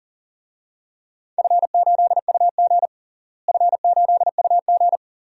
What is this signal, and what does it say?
Morse code, heard as a keyed tone, about 40 wpm, 700 Hz. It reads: F8UG F8UG
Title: DXCC-100 Call Signs - 4 Characters - Repeated 2x - 40wpm